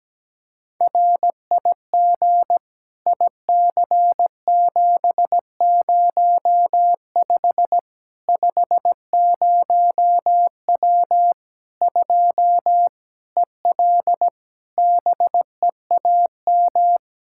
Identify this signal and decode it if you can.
Morse code, heard as a keyed tone, about 17 wpm, 700 Hz. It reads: RIG IC705 50W 2 EL BEAM
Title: Practice QSO - Exchange 2 - Info - 17wpm